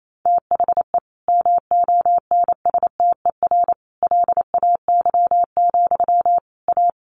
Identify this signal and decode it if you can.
Morse code, heard as a keyed tone, about 28 wpm, 700 Hz. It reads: THE MONSTER LAY, A
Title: A Sound Of Thunder 28wpm